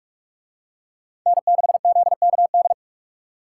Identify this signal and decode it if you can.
Morse code, heard as a keyed tone, about 41 wpm, 700 Hz. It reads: N6ZKD